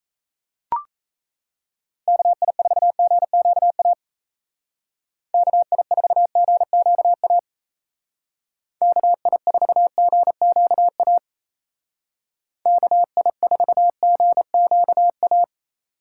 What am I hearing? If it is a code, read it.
Morse code, heard as a keyed tone, about 34 wpm, 700 Hz. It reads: KI4GQA KI4GQA KI4GQA KI4GQA
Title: Speed Racing - USA 2x3 Call Signs 28wpm